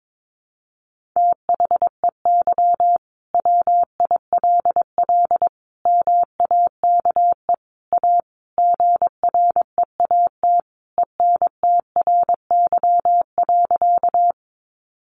Morse code, heard as a keyed tone, about 22 wpm, 700 Hz. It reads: THEY WILL MAKE A GREAT ENTRY.
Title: Sentences from Top 1500 Words 22wpm